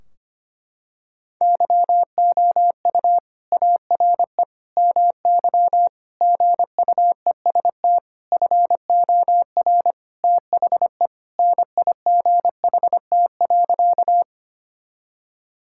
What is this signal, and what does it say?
Morse code, heard as a keyed tone, about 25 wpm, 700 Hz. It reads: YOU ARE MY GUEST FOR THE NIGHT.
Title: Sentences from Top 1600 Words 25wpm